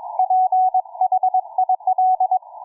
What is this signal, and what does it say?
Morse code, heard as a keyed tone, about 22 wpm, 740 Hz, with radio noise. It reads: PHIL